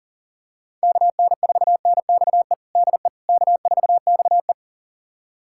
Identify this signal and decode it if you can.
Morse code, heard as a keyed tone, about 40 wpm, 700 Hz. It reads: KN4NXE DE K4XE
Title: USA Callsign DE Callsign 40wpm